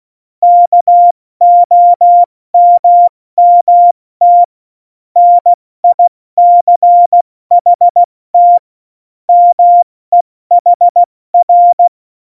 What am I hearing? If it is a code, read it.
Morse code, heard as a keyed tone, about 16 wpm, 700 Hz. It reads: KOMMT NICHT MEHR
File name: Morsekurs / Witze / Witz 18. September 2018, 70 F80